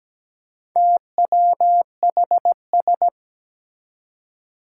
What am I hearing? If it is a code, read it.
Morse code, heard as a keyed tone, about 17 wpm, 700 Hz. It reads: TWHS
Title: Four Challenging Letter-Number Combinations 17wpm